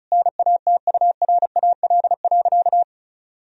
Morse code, heard as a keyed tone, about 34 wpm, 700 Hz. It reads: NATURAL.
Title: Sentences from Top 1000 Words - Review Part 4 - 35wpm